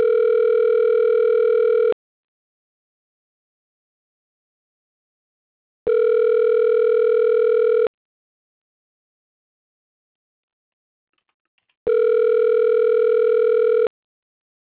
Ringback tone (North America), 440 + 480 Hz.